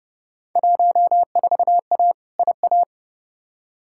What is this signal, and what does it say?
Morse code, heard as a keyed tone, about 32 wpm, 700 Hz. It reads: 14A IA
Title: Field Day Exchanges Encore - 30wpm